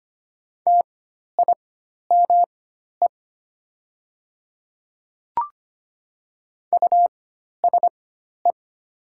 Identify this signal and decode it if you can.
Morse code, heard as a keyed tone, about 25 wpm, spaced as for 12 wpm, 700 Hz. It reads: TIME USE
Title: 4x Character Spacing - Top 100 Words - 25wpm